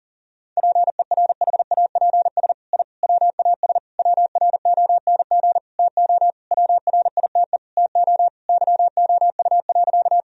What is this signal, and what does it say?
Morse code, heard as a keyed tone, about 40 wpm, 700 Hz. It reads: PERHAPS I WAS WRONG TO WRITE TO YOU.
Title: Sentences from Top 1000 Words - Review Part 1 - 40wpm